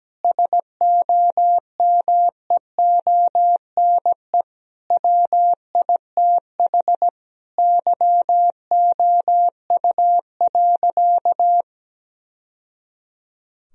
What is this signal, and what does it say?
Morse code, heard as a keyed tone, about 17 wpm, 700 Hz. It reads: SOMEONE WITH YOU.
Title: Sentences from Top 1000 Words - Review Part 4 - 17wpm